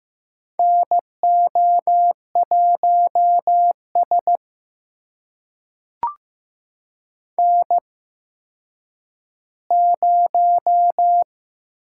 Morse code, heard as a keyed tone, about 15 wpm, spaced as for 9 wpm, 700 Hz. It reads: NO1S N0
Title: USA Call Signs - 4 Characters - 8x Character Spacing - 15wpm